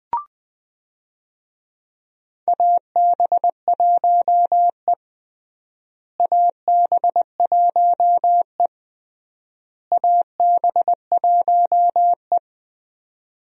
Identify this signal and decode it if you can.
Morse code, heard as a keyed tone, about 21 wpm, 700 Hz. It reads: AB1E AB1E AB1E
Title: USA Call Signs - 4 Characters - Repeated 3x - 20wpm